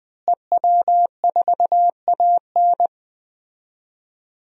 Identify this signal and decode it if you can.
Morse code, heard as a keyed tone, about 21 wpm, 700 Hz. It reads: EW4AN
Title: DXCC-100 Call Signs - 5 Characters - Repeated 3x - 20wpm